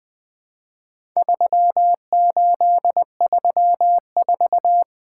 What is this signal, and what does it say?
Morse code, heard as a keyed tone, about 20 wpm, 700 Hz. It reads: 3834
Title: Four Letter-Number Combinations - 20wpm